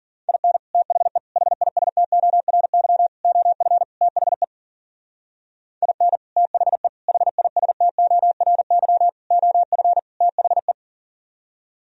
Morse code, heard as a keyed tone, about 46 wpm, 700 Hz. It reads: IN THE HISTORY OF THE IN THE HISTORY OF THE
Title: Speed-Racing - Sets of 5 Words 40wpm